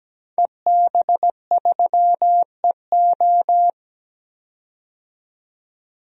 Morse code, heard as a keyed tone, about 17 wpm, 700 Hz. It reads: EB3EO
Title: DXCC-100 Call Signs - 5 Characters - 17wpm